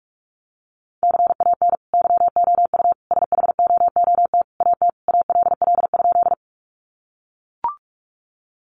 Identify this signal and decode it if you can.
Morse code, heard as a keyed tone, about 44 wpm, 700 Hz. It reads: CAN YOU SHOOT AT ALL?
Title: Sentences from Top 1200 Words 45wpm